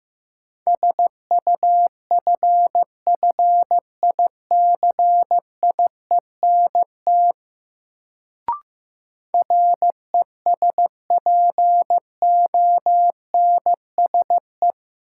Morse code, heard as a keyed tone, about 15 wpm, 700 Hz. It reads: SUFFICIENT RESPONSE
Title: Sentences from Top 1400 Words - New Word Drill - 15wpm